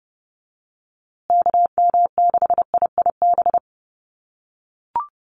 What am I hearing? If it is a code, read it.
Morse code, heard as a keyed tone, about 30 wpm, 700 Hz. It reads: KM6IIB
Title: USA Call Signs Repeated 2x 30wpm